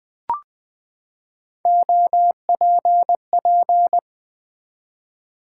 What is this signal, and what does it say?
Morse code, heard as a keyed tone, about 20 wpm, 700 Hz. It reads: OPP
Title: Build A Word - 3 Letters At A Time - 8-Letter Words - 20wpm